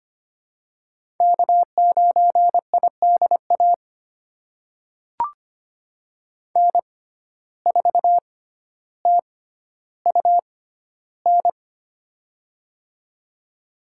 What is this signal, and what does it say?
Morse code, heard as a keyed tone, about 25 wpm, spaced as for 13 wpm, 700 Hz. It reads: K9IDA N4TUN
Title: USA Call Signs - 6x Character Spacing - 25wpm